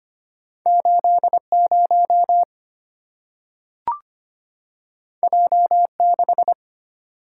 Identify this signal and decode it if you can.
Morse code, heard as a keyed tone, about 25 wpm, 700 Hz. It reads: 80 J6
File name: Two Letter-Number Combinations - 25wpm